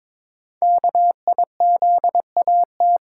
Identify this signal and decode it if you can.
Morse code, heard as a keyed tone, about 23 wpm, 700 Hz. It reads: KIZAT
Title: Code Groups - 5 Letters - 22wpm